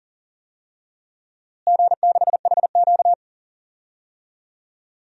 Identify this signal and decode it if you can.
Morse code, heard as a keyed tone, about 40 wpm, 700 Hz. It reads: G6HQ
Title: DXCC-100 Call Signs - 4 Characters - Repeated 3x - 40wpm